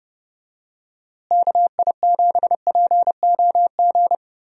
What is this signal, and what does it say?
Morse code, heard as a keyed tone, about 29 wpm, 700 Hz. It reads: KI7POG
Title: USA Call Signs - 6 Characters - Repeated 3x - 30wpm